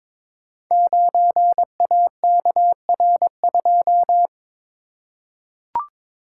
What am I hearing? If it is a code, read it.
Morse code, heard as a keyed tone, about 22 wpm, 700 Hz. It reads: 9AKR2
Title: Code Groups - 5 Letter:Numbers - 22wpm